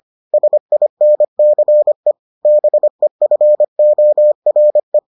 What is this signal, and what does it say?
Morse code, heard as a keyed tone, about 25 wpm, 600 Hz. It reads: SINCE BEFORE